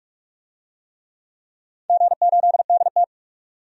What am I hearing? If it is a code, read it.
Morse code, heard as a keyed tone, about 44 wpm, 700 Hz. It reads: G8DT
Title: DXCC-100 Call Signs - 4 Characters - 45wpm